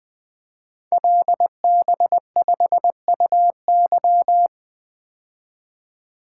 Morse code, heard as a keyed tone, about 21 wpm, 700 Hz. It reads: LB5UY
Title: Code Groups - 5 Letter:Numbers - 20wpm